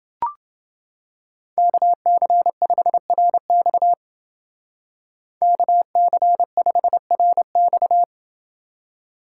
Morse code, heard as a keyed tone, about 29 wpm, 700 Hz. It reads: KC5RX KC5RX